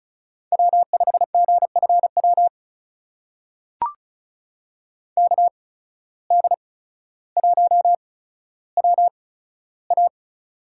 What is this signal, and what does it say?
Morse code, heard as a keyed tone, about 34 wpm, spaced as for 15 wpm, 700 Hz. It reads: W5GFW KD1WA
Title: USA Call Signs - 5 Characters - 8x Character Spacing - 35wpm